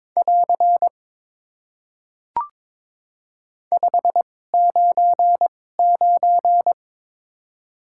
Morse code, heard as a keyed tone, about 22 wpm, spaced as for 16 wpm, 700 Hz. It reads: <AR> 599
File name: Top 90 QSO Elements - 2x Character Spacing - 22wpm